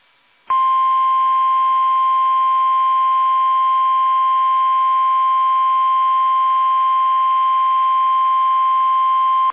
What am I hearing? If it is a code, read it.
Emergency Alert System (SAME) tones (the 1050 Hz attention tone). It is heard from 1.1 to 9.1 s.
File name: eas / KUSO / KUSO